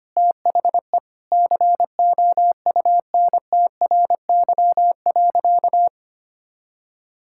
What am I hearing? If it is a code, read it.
Morse code, heard as a keyed tone, about 25 wpm, 700 Hz. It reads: THE COUNTRY.
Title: Sentences from Top 1000 Words - Review Part 5 - 25wpm